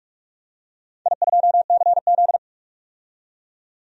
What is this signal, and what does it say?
Morse code, heard as a keyed tone, about 46 wpm, 700 Hz. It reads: I2CZ
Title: DXCC-100 Call Signs - 4 Characters - 45wpm